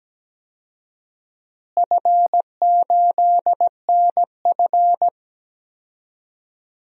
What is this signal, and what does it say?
Morse code, heard as a keyed tone, about 17 wpm, 700 Hz. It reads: F8NF
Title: DXCC-100 Call Signs - 4 Characters - Repeated 3x - 17wpm